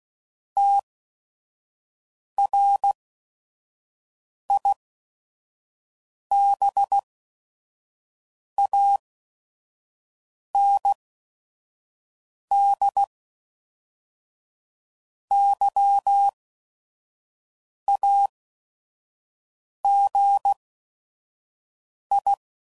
Morse code, heard as a keyed tone, about 16 wpm, spaced as for 5 wpm, 800 Hz. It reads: TRIBAND YAGI
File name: CW Code Groups @5WPM / fromSM's / Antennas